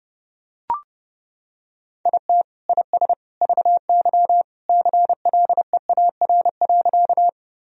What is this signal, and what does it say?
Morse code, heard as a keyed tone, about 32 wpm, 700 Hz. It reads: IT IS VY CLEAR.
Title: Sentences from Top 500 Words and Abbreviations 30wpm